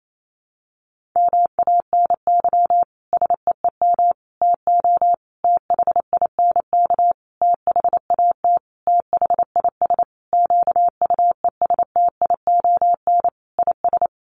Morse code, heard as a keyed tone, about 28 wpm, 700 Hz. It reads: MANY SEEM TO THINK THAT THIS QUESTION IS
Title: Sentences from Top 1400 Words 28wpm